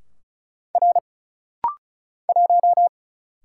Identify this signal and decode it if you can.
Morse code, heard as a keyed tone, about 36 wpm, 700 Hz. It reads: R 1
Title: Single Letter-Number Alt - Rapid-Fire 35wpm